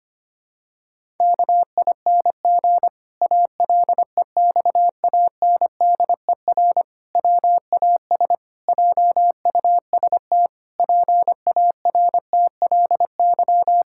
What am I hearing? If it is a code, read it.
Morse code, heard as a keyed tone, about 25 wpm, 700 Hz. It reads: KING ALEXANDER WAS JUST PARTLY
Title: Pangram Challenge v2 - 25wpm